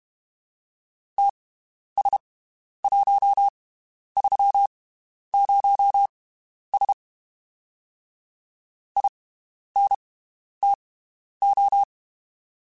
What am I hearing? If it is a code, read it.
Morse code, heard as a keyed tone, about 32 wpm, spaced as for 11 wpm, 800 Hz. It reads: TS130S INTO
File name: (spaced) / QSOs Spaced-mp3s / WD0BGP 1 @30 sp